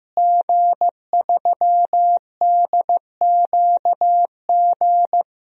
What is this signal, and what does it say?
Morse code, heard as a keyed tone, about 15 wpm, 700 Hz. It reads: G3DQG
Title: DXCC-100 Call Signs - 5 Characters - Repeated 3x - 15wpm